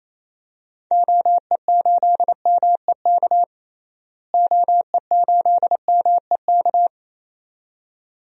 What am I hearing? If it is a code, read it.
Morse code, heard as a keyed tone, about 28 wpm, 700 Hz. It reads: OE8MEK OE8MEK
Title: DXCC-100 Call Signs - 6 Characters - Repeated 2x - 28wpm